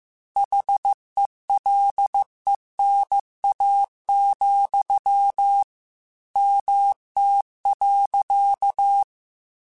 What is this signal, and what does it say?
Morse code, heard as a keyed tone, about 15 wpm, 800 Hz. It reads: HELENA, MT.
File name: QSOs / QSOs-mp3s / WD0BGP 1@10